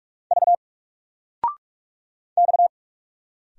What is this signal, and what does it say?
Morse code, heard as a keyed tone, about 44 wpm, 700 Hz. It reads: V X
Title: Single Letter-Number Alt 45wpm